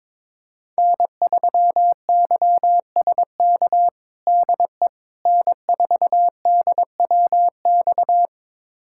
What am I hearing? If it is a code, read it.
Morse code, heard as a keyed tone, about 21 wpm, 700 Hz. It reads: N3YSK DE N4DWX
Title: USA Callsign DE Callsign - 5 Characters - 22wpm